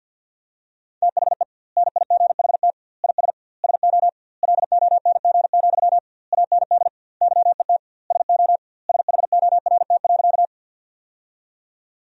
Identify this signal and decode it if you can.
Morse code, heard as a keyed tone, about 51 wpm, 700 Hz. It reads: THE NIGHT IS SO LONG, AND YET SO SHORT.
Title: 2x Word-Spacing - Sentences from Top 500 Words - 50wpm